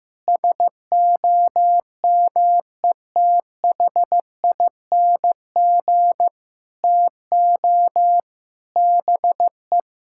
Morse code, heard as a keyed tone, about 15 wpm, 700 Hz. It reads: SOMETHING TO BE
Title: Sentences from Top 1000 Words - Review Part 6 - 15wpm